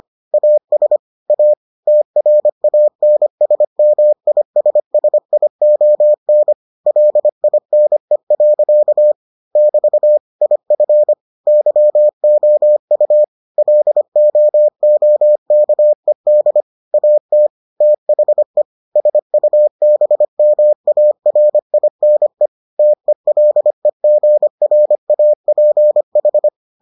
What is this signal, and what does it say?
Morse code, heard as a keyed tone, about 25 wpm, 600 Hz. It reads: AS A TRANSMISSION LINE. <BT> IF YOU LOOKED AT THE SUBMARINE TELEGRAPH